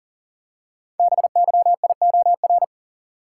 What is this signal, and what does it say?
Morse code, heard as a keyed tone, about 38 wpm, 690 Hz. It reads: BYIOR
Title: Code Groups - 5 Letters - 40wpm